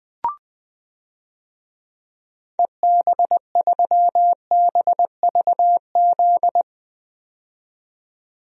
Morse code, heard as a keyed tone, about 20 wpm, 700 Hz. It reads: EB3BVZ
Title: DXCC-100 Call Signs - 6 Characters - 20wpm